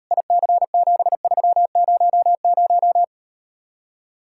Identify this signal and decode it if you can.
Morse code, heard as a keyed tone, about 38 wpm, 700 Hz. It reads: IC7300